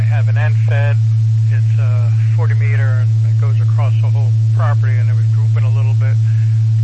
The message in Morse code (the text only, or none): none